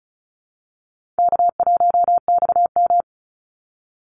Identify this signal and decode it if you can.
Morse code, heard as a keyed tone, about 34 wpm, 700 Hz. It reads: K1XM
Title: Call Signs Repeated 3x - 35wpm